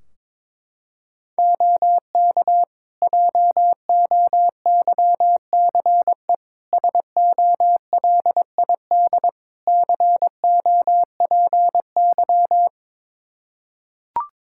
Morse code, heard as a keyed tone, about 22 wpm, 700 Hz. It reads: OK JOYCE SOLID COPY
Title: Practice QSO - Exchange 3 - More Info - 22wpm